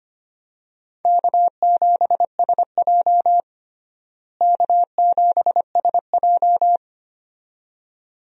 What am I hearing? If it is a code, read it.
Morse code, heard as a keyed tone, about 25 wpm, 700 Hz. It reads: K7SJ K7SJ
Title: USA Call Signs - 4 Characters - Repeated 2x - 25wpm